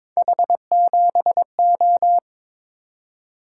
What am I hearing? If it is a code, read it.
Morse code, heard as a keyed tone, about 21 wpm, 700 Hz. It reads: H7O
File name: practice / 22wpm / Three Letter-Number Combinations - 22wpm